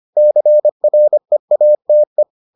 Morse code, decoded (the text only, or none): CREATE